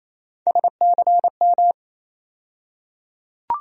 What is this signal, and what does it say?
Morse code, heard as a keyed tone, about 28 wpm, 700 Hz. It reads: SCM
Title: USA Call Sign Postfixes - 3 Letters 28wpm